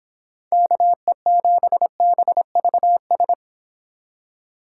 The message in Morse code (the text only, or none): KE7BVS